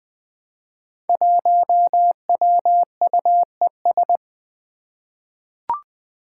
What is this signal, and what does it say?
Morse code, heard as a keyed tone, about 20 wpm, 700 Hz. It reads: 1WUES
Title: Five Challenging Letter-Number Combinations 20wpm